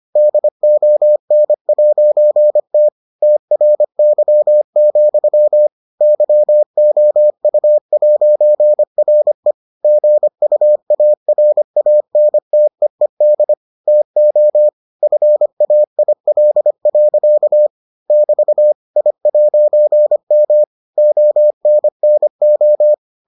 Morse code, heard as a keyed tone, about 25 wpm, 600 Hz. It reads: DON'T TRY, YOU'RE GUARANTEED TO FAIL. <BT> I'M ONNO